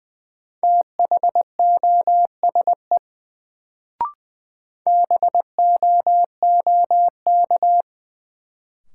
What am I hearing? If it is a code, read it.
Morse code, heard as a keyed tone, about 20 wpm, 700 Hz. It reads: THOSE BOOK
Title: Sentences from Top 300 Words - New Word Drill - 20wpm